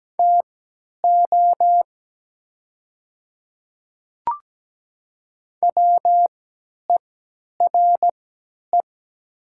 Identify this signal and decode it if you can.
Morse code, heard as a keyed tone, about 17 wpm, spaced as for 10 wpm, 700 Hz. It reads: TO WERE
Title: 3x Character Spacing - Top 100 Words - 17wpm